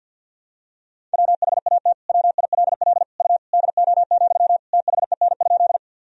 Morse code, heard as a keyed tone, about 53 wpm, 700 Hz. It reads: WHAT WILL U DO, THEN?
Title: Sentences from Top 100 Words and Abbreviations 50wpm